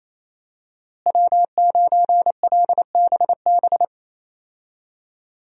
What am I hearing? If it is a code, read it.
Morse code, heard as a keyed tone, about 27 wpm, 700 Hz. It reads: W9LBB
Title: USA Call Signs - 5 Characters - Repeated 3x - 28wpm